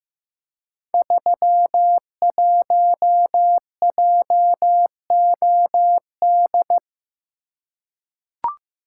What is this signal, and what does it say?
Morse code, heard as a keyed tone, about 15 wpm, 700 Hz. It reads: 31JOD
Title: Code Groups - 5 Letter:Numbers - 15wpm